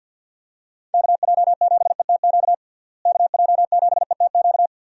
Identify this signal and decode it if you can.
Morse code, heard as a keyed tone, about 48 wpm, 700 Hz. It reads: KJ7ETQ KJ7ETQ
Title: USA Call Signs - 6 Characters - Repeated 2x - 50wpm